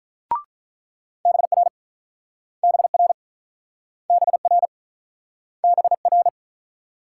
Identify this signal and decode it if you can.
Morse code, heard as a keyed tone, about 48 wpm, 700 Hz. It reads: BR BR BR BR